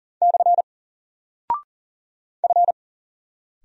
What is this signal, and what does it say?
Morse code, heard as a keyed tone, about 40 wpm, 700 Hz. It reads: / F
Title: Single Character (Letter:Number:Punctuation:Prosign) 40wpm